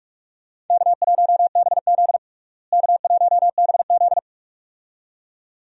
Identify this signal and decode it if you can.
Morse code, heard as a keyed tone, about 46 wpm, 700 Hz. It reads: K1BZ K1BZ
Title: USA Call Signs - 4 Characters - Repeated 2x - 45wpm